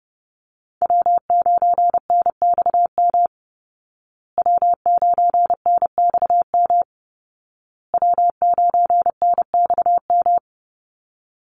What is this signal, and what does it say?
Morse code, heard as a keyed tone, about 30 wpm, 700 Hz. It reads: W9NXM W9NXM W9NXM
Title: Call Signs Repeated 2x - 30wpm